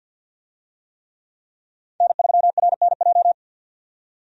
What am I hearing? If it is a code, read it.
Morse code, heard as a keyed tone, about 48 wpm, 700 Hz. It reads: N3RNJ
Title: USA Call Signs - 5 Characters - 50wpm